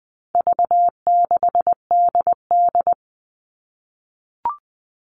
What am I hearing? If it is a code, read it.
Morse code, heard as a keyed tone, about 20 wpm, 700 Hz. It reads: V6DD
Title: All Cut Numbers - 4-Digit 20wpm